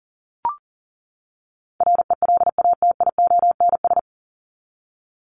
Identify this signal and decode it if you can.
Morse code, heard as a keyed tone, about 40 wpm, 700 Hz. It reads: RELATIONS
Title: ES:S Suffix - Top 100 Words with Base Words 40wpm